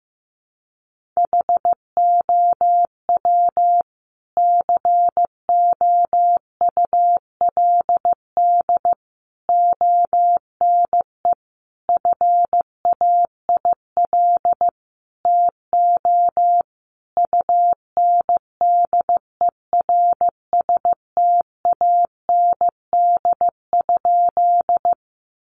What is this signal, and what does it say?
Morse code, heard as a keyed tone, about 15 wpm, 700 Hz. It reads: HOW COULD ONE FAIL TO UNDERSTAND?
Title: Sentences from Top 1100 Words 15wpm